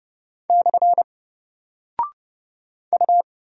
Morse code, heard as a keyed tone, about 30 wpm, 700 Hz. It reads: / U
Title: Single Character (Letter:Number:Punctuation:Prosign) 30wpm